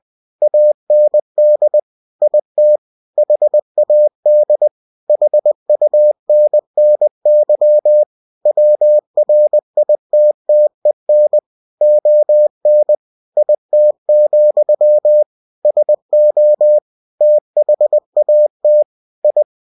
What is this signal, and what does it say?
Morse code, heard as a keyed tone, about 20 wpm, 600 Hz. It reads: AND IT HAD HUNNY WRITTEN ON IT, SO THAT I
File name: cw-books / winnie-the-pooh / winnie-20wpm-page-0039